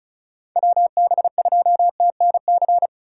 Morse code, heard as a keyed tone, about 36 wpm, 690 Hz. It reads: WB2TNC
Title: USA Call Signs - 6 Characters - 3x Character Spacing - 35wpm